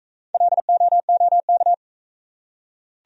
Morse code, heard as a keyed tone, about 41 wpm, 700 Hz. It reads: LOOK